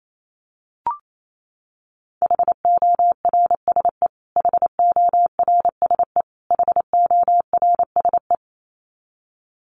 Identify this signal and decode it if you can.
Morse code, heard as a keyed tone, about 28 wpm, 700 Hz. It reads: HORSE HORSE HORSE
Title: Top 500 Words Repeated 28wpm